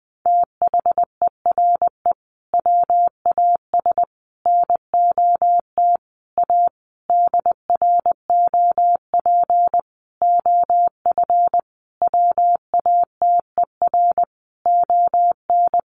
Morse code, heard as a keyed tone, about 20 wpm, 700 Hz. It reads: THERE WAS NOT A DROP OF WATER ON
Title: Sentences from Top 1400 Words 20wpm